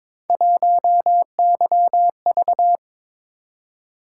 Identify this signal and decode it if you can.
Morse code, heard as a keyed tone, about 23 wpm, 700 Hz. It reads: 1YV
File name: Three Letter-Number Combinations - 22wpm